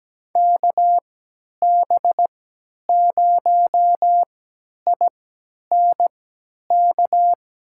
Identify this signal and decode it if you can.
Morse code, heard as a keyed tone, about 17 wpm, spaced as for 10 wpm, 700 Hz. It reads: KB0INK
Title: USA Call Signs - 6 Characters - 3x Character Spacing - 17wpm